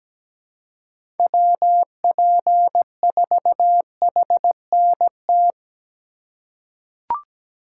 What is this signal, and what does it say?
Morse code, heard as a keyed tone, about 17 wpm, 700 Hz. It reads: WP4HNT
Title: USA Call Signs - 6 Characters - 4x Character Spacing - 17wpm